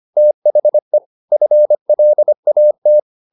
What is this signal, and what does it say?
Morse code, heard as a keyed tone, about 25 wpm, 600 Hz. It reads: THE FLAT
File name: the old man and the sea part 1 25wpm